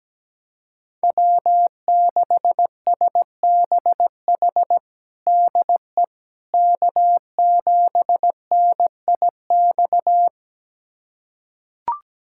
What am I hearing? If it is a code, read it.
Morse code, heard as a keyed tone, about 17 wpm, 700 Hz. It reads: W6SBH DE K7NIX
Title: USA Callsign DE Callsign 17wpm